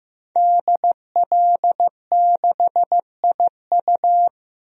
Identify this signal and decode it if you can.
Morse code, heard as a keyed tone, about 15 wpm, 700 Hz. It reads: DL6IU
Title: DXCC-100 Call Signs - 5 Characters - 15wpm